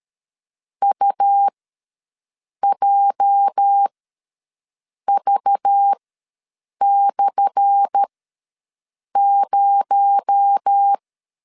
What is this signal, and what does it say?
Morse code, heard as a keyed tone, about 12 wpm, spaced as for 6 wpm, 800 Hz. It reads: UJV/0
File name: LEZIONI / 10 SIGNS / 10C DOT ws 12wpm 3548